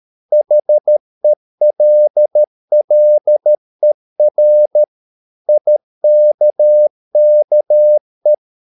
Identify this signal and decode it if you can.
Morse code, heard as a keyed tone, about 13 wpm, 600 Hz. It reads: HELLER IKKE
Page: Ferie 13 wpm